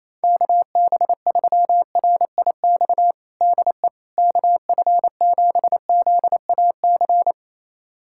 Morse code, heard as a keyed tone, about 28 wpm, 700 Hz. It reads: KB3RIX DE KF7ZAC
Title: USA Callsign DE Callsign - 6 Characters - 28wpm